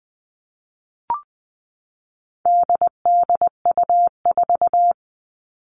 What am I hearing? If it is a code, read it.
Morse code, heard as a keyed tone, about 20 wpm, 700 Hz. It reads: DDU4
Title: All Cut Numbers - 4-Digit 20wpm